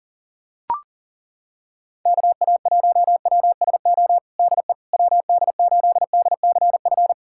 Morse code, heard as a keyed tone, about 37 wpm, 690 Hz. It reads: KA1WSO DE WD8DCF
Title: USA Callsign DE Callsign - 6 Characters - 40wpm